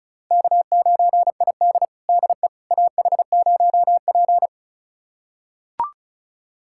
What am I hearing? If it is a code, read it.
Morse code, heard as a keyed tone, about 36 wpm, 700 Hz. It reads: K9ID DE AH0P
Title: USA Callsign DE Callsign - 4 Characters - 35wpm